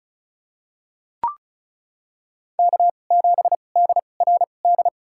Morse code, heard as a keyed tone, about 36 wpm, spaced as for 25 wpm, 700 Hz. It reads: K7DRD
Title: USA Call Signs - 5 Characters - 2x Character Spacing - 35wpm